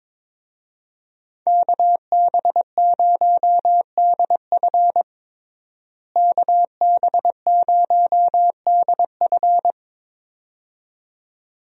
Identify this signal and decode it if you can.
Morse code, heard as a keyed tone, about 22 wpm, 700 Hz. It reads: KB0DF KB0DF
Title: USA Call Signs - 5 Characters - Repeated 2x - 22wpm